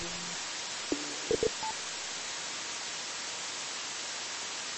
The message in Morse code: none

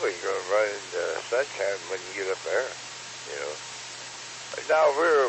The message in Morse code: none

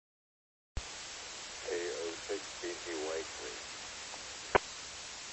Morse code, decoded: none

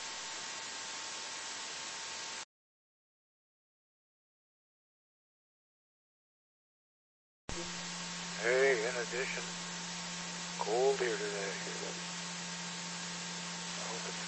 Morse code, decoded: none

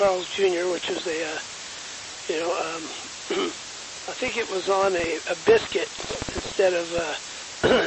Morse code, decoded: none

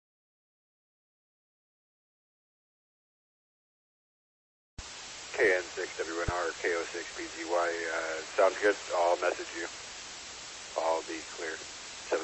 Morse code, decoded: none

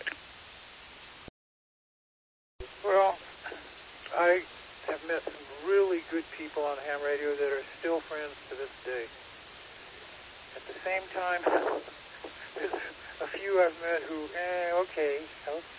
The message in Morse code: none

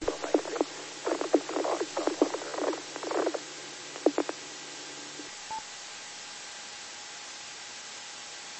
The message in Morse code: none

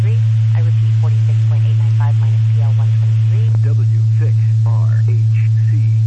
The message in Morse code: none

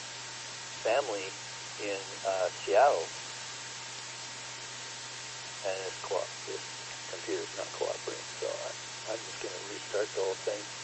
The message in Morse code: none

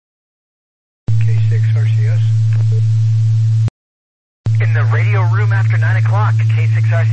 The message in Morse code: none

